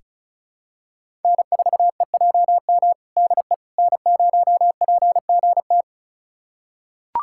Morse code, heard as a keyed tone, about 36 wpm, 700 Hz. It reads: N4EJM DE N0PGT
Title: USA Callsign DE Callsign - 5 Characters - 35wpm